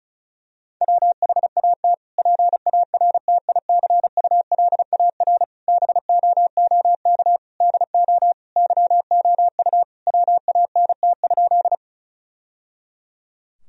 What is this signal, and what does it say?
Morse code, heard as a keyed tone, about 34 wpm, 700 Hz. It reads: WHAT PARTICULAR BOOK DO YOU WANT?
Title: Sentences from Top 1000 Words - Review Part 4 - 35wpm